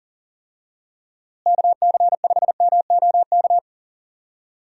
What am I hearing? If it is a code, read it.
Morse code, heard as a keyed tone, about 40 wpm, 700 Hz. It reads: KC5MOK
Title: USA Call Signs - 6 Characters - Repeated 3x - 40wpm